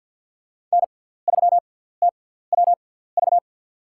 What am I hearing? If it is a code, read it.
Morse code, heard as a keyed tone, about 48 wpm, spaced as for 17 wpm, 700 Hz. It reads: N3TWV
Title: USA Call Signs - 5 Characters - 6x Character Spacing - 50wpm